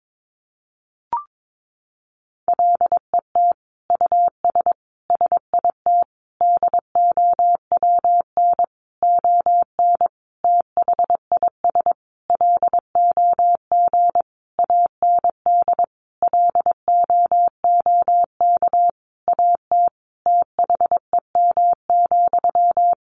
Morse code, heard as a keyed tone, about 22 wpm, 700 Hz. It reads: LET US SIT DOWN ON THIS LOG AND LOOK AT THEM,
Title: Sentences from Top 1400 Words 22wpm